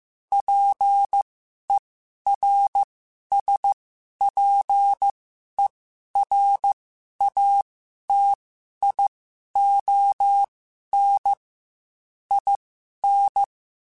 Morse code, heard as a keyed tone, about 15 wpm, spaced as for 11 wpm, 800 Hz. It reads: PERSPERATION IN
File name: Quotes / Quotes-mp3s / Persperation @13